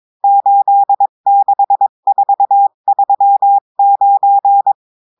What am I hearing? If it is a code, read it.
Morse code, heard as a keyed tone, about 22 wpm, 800 Hz. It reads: 86439
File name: cours-du-mercredi / audios / cours-853-du-2-mai-2024-etxte chiffre et texte clair-a-20-wpm